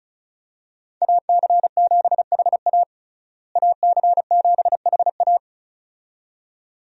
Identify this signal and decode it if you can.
Morse code, heard as a keyed tone, about 36 wpm, 700 Hz. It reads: AC7HA AC7HA
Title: USA Call Signs - 5 Characters - Repeated 2x - 35wpm